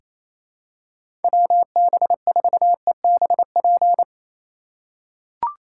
Morse code, heard as a keyed tone, about 28 wpm, 700 Hz. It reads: WB4EBP
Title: USA Call Signs - 6 Characters - 2x Character Spacing - 28wpm